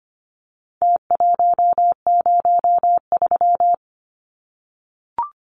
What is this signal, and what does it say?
Morse code, heard as a keyed tone, about 25 wpm, 700 Hz. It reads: T103
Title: Numbers - Contest Style - 4-Digit 25wpm